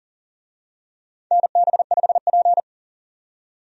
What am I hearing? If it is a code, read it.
Morse code, heard as a keyed tone, about 40 wpm, 700 Hz. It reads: NB5P
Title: USA Call Signs - 4 Characters - 40wpm